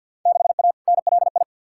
Morse code, heard as a keyed tone, about 52 wpm, 700 Hz. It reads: 6A NLI
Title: Field Day Exchanges Encore - 50wpm